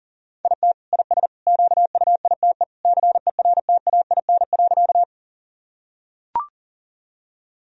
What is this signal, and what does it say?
Morse code, heard as a keyed tone, about 40 wpm, 700 Hz. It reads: IT IS QUITE CERTAIN.
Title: Sentences from Top 1000 Words - Review Part 2 - 40wpm